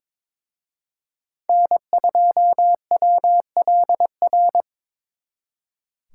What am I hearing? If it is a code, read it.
Morse code, heard as a keyed tone, about 25 wpm, 700 Hz. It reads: N2WLR
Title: Speed Racing - USA 1x3 Call Signs 22wpm